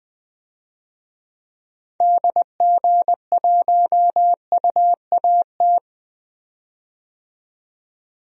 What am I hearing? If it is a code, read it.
Morse code, heard as a keyed tone, about 21 wpm, 700 Hz. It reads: DG1UAT
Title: DXCC-100 Call Signs - 6 Characters - 20wpm